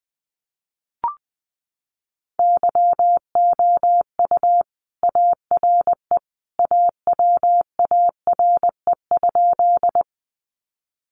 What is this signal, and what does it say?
Morse code, heard as a keyed tone, about 20 wpm, 700 Hz. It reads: YOU ARE AWARE?
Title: Sentences from Top 1200 Words 20wpm